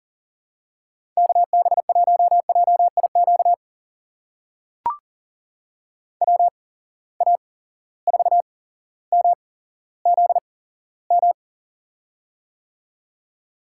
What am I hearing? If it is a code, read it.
Morse code, heard as a keyed tone, about 40 wpm, spaced as for 17 wpm, 700 Hz. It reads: KB1JIQ WA4MZM
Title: USA Call Signs - 6 Characters - 8x Character Spacing - 40wpm